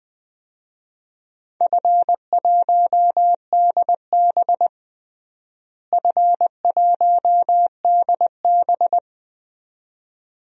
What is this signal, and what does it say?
Morse code, heard as a keyed tone, about 20 wpm, 700 Hz. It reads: F1DB F1DB
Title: DXCC-100 Call Signs - 4 Characters - Repeated 2x - 20wpm